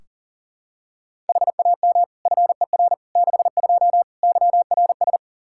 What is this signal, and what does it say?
Morse code, heard as a keyed tone, about 37 wpm, 700 Hz. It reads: HAM FER 62 YRS
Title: K1USN SST QSO Practice 40wpm